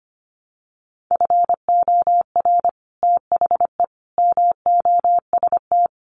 Morse code, heard as a keyed tone, about 25 wpm, 700 Hz. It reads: FOR THE MOST
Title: Sets of 4 Words 25wpm